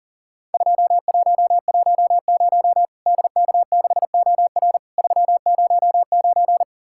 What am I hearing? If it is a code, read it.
Morse code, heard as a keyed tone, about 40 wpm, 700 Hz. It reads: 2110 DK6OR 309
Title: Worked All Europe DX Contest - QTC Exchange - 40wpm